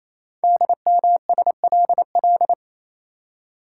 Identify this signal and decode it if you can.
Morse code, heard as a keyed tone, about 27 wpm, 700 Hz. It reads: DMSLL
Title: Code Groups - 5 Letter:Numbers - 28wpm